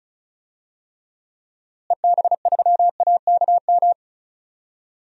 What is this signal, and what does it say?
Morse code, heard as a keyed tone, about 34 wpm, 700 Hz. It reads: EB3AKM
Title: DXCC-100 Call Signs - 6 Characters - Repeated 3x - 35wpm